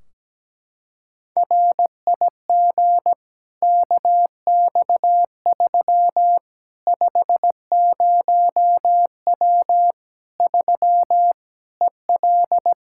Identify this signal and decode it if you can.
Morse code, heard as a keyed tone, about 17 wpm, 700 Hz. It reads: RIG KX3 50W 3 EL
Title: Practice QSO - Exchange 2 - Info - 17wpm